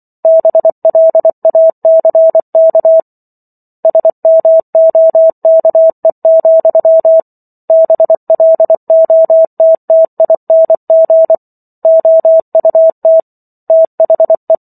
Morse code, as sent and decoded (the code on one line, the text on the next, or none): -... .-.. .- -.-. -.- / ... -- --- -.- . --..-- / -... .-.. --- - - .. -. --. / --- ..- - / - .... .
BLACK SMOKE, BLOTTING OUT THE